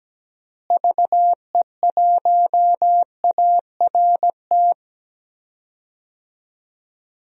Morse code, heard as a keyed tone, about 17 wpm, 700 Hz. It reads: VE1ART
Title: DXCC-100 Call Signs - 6 Characters - Repeated 3x - 17wpm